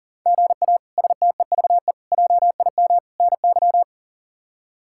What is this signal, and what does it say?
Morse code, heard as a keyed tone, about 40 wpm, 700 Hz. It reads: GA STEVE JIM NY
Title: K1USN SST QSO Practice - 40wpm